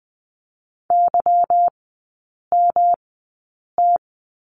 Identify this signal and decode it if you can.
Morse code, heard as a keyed tone, about 20 wpm, spaced as for 8 wpm, 700 Hz. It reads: YMT
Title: 152 Lesson 25e - Letter p - Three Characters 20wpm